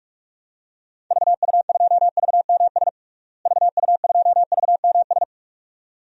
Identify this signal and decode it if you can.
Morse code, heard as a keyed tone, about 46 wpm, 700 Hz. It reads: VU2VMS VU2VMS
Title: DXCC-100 Call Signs - 6 Characters - Repeated 2x - 45wpm